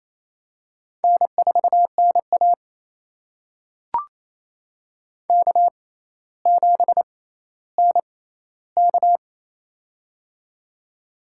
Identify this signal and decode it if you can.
Morse code, heard as a keyed tone, about 28 wpm, spaced as for 14 wpm, 700 Hz. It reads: N4NA K7NK
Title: USA Call Signs - 4 Characters - 6x Character Spacing - 28wpm